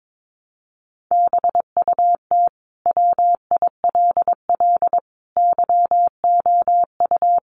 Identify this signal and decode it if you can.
Morse code, heard as a keyed tone, about 22 wpm, 700 Hz. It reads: BUT WILL YOU
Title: Sentences from Top 1400 Words 22wpm